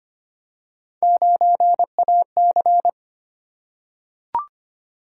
Morse code, heard as a keyed tone, about 25 wpm, 700 Hz. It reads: 9AC
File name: Three Letter-Number Combinations - 25wpm